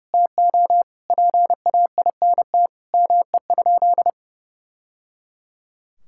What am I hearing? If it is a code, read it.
Morse code, heard as a keyed tone, about 28 wpm, 700 Hz. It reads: TO PAINT ME?
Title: Sentences from Top 1000 Words - Review Part 6 - 30wpm